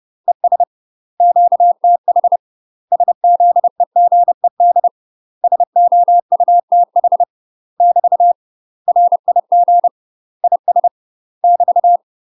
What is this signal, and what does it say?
Morse code, heard as a keyed tone, about 30 wpm, 700 Hz. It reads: ES QTH SZEGED SOUTH <BT> RIG IS <BT>
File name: UK MASTER QSOs TRAINING/30WPM